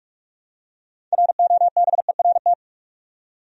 Morse code, heard as a keyed tone, about 43 wpm, 700 Hz. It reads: ROBERT
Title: USA Common Names - 45wpm